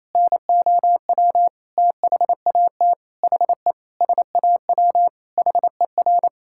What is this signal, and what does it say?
Morse code, heard as a keyed tone, about 28 wpm, 700 Hz. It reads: NOW THAT HE SAW HER
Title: Sentences from Top 1000 Words - Review Part 6 - 28wpm